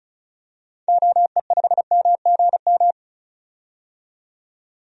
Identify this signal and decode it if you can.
Morse code, heard as a keyed tone, about 34 wpm, 700 Hz. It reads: OE5MGM
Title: DXCC-100 Call Signs - 6 Characters - 35wpm